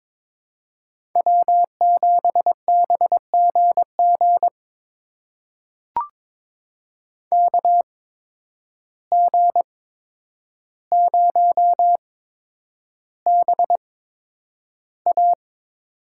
Morse code, heard as a keyed tone, about 22 wpm, spaced as for 9 wpm, 700 Hz. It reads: W7BGG KG0BA
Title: USA Call Signs - 5 Characters - 8x Character Spacing - 22wpm